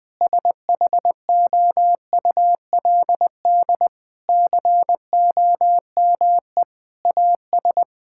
Morse code, heard as a keyed tone, about 20 wpm, 700 Hz. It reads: SHOULD COME AS
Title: Sentences with 8-Word Phrases - 20wpm